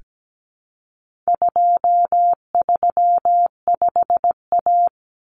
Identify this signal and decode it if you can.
Morse code, heard as a keyed tone, about 17 wpm, 700 Hz. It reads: 235A
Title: Common Cut Numbers - 4-Digit 17wpm